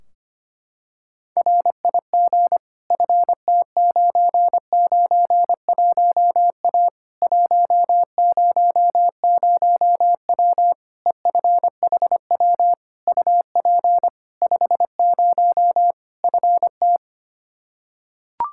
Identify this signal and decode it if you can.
Morse code, heard as a keyed tone, about 25 wpm, 700 Hz. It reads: RIG FT991A 100W EFHW UP 50 FT
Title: Practice QSO - Exchange 2 - Info - 25wpm